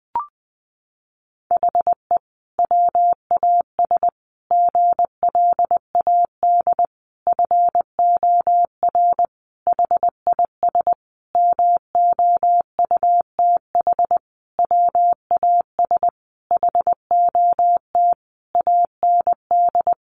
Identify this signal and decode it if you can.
Morse code, heard as a keyed tone, about 20 wpm, 700 Hz. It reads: HE WAS GLAD FOR HIS MOUTH WAS HOT AND
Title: Sentences from Top 1500 Words 20wpm